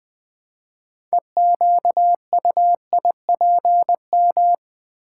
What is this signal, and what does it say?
Morse code, heard as a keyed tone, about 20 wpm, 700 Hz. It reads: EQUIPM
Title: Build A Word - 3 Letters At A Time - 9-Letter Words - 20wpm